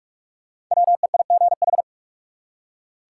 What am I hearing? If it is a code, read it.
Morse code, heard as a keyed tone, about 46 wpm, 700 Hz. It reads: WEIGH
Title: Five Letter Words - 45wpm